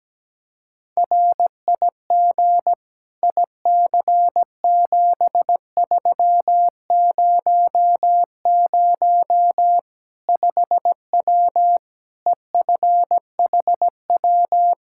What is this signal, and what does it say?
Morse code, heard as a keyed tone, about 17 wpm, 700 Hz. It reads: RIG IC7300 5W EFHW
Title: Practice QSO - Exchange 2 - Info - 17wpm